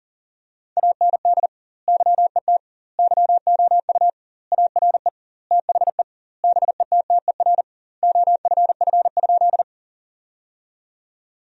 Morse code, heard as a keyed tone, about 40 wpm, 700 Hz. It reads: AND YET YOU ARE THE BETTER OFF?
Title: 2x Word-Spacing - Sentences from Top 500 Words - 40wpm